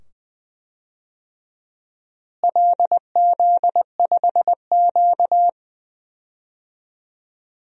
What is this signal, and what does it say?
Morse code, heard as a keyed tone, about 20 wpm, 700 Hz. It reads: LZ5Q
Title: DXCC-100 Call Signs - 4 Characters - Repeated 2x - 20wpm